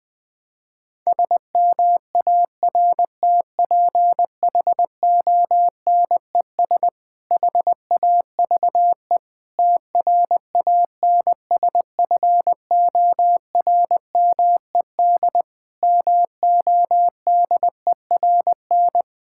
Morse code, heard as a keyed tone, about 21 wpm, 700 Hz. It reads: SMARTPHONES HAVE TRANSFORMED MODERN
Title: Difficult Sentences with Hints - 20wpm